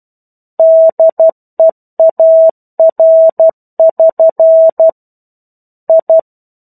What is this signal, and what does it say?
Morse code, heard as a keyed tone, about 12 wpm, 650 Hz. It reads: DEAR<SN> I
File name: 01-Chapter-01 12 WPM